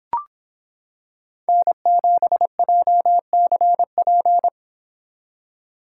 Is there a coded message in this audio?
Yes: Morse code from 1.5 to 4.5 s, heard as a keyed tone, about 26 wpm, 700 Hz. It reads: N7JCP